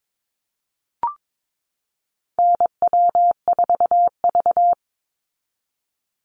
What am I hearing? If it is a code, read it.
Morse code, heard as a keyed tone, about 22 wpm, 700 Hz. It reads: NW4V
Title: Call Signs Repeated 3x - 22wpm